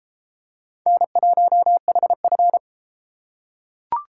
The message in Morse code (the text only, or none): N1HF